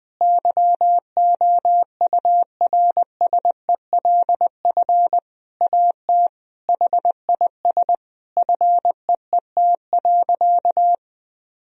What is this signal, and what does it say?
Morse code, heard as a keyed tone, about 21 wpm, 700 Hz. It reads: YOURSELF AT HIS FEET.
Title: Sentences from Top 1000 Words - Review Part 1 - 20wpm